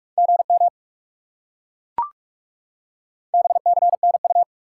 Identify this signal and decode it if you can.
Morse code, heard as a keyed tone, about 37 wpm, 700 Hz. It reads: GM BCNU
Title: Speed-Racing Top 90 QSO Elements 30wpm